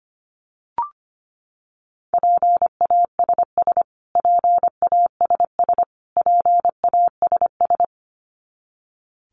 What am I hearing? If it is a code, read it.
Morse code, heard as a keyed tone, about 25 wpm, 700 Hz. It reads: PASS PASS PASS
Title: Top 500 Words Repeated 25wpm